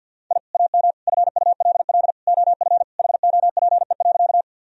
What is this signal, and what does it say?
Morse code, heard as a keyed tone, about 50 wpm, 700 Hz. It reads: I AM FULL OF HOPE.
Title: Sentences from Top 1000 Words - Review Part 3 - 50wpm